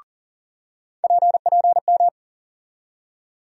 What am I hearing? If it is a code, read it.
Morse code, heard as a keyed tone, about 40 wpm, 700 Hz. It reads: PPM
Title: USA Call Sign Postfixes - 3 Letters 40wpm